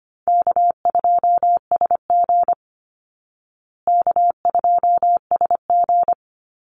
Morse code, heard as a keyed tone, about 25 wpm, 700 Hz. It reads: K2SG K2SG
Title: Easy Call Signs Repeated 2x - 25wpm